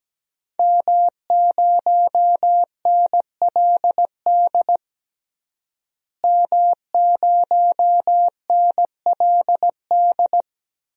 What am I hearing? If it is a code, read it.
Morse code, heard as a keyed tone, about 17 wpm, 700 Hz. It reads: M0NLD M0NLD
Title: DXCC-100 Call Signs - 5 Characters - Repeated 2x - 17wpm